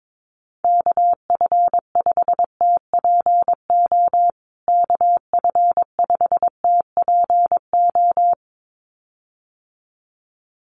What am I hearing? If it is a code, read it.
Morse code, heard as a keyed tone, about 22 wpm, 700 Hz. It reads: KF5TPO KF5TPO
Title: USA Call Signs Repeated 2x 22wpm